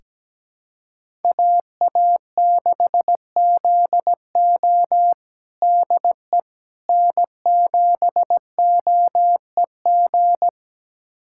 Morse code, heard as a keyed tone, about 17 wpm, 700 Hz. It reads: AA6ZO DE N7OEG
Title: USA Callsign DE Callsign - 5 Characters - 17wpm